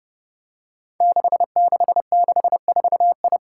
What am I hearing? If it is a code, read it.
Morse code, heard as a keyed tone, about 30 wpm, 700 Hz. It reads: 6664I
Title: Five Challenging Letter-Number Combinations 30wpm